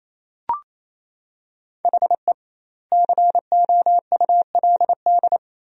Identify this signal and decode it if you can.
Morse code, heard as a keyed tone, about 28 wpm, 700 Hz. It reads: HE COULD
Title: 2x Word-Spacing - Sentences from Top 300 Words - 28wpm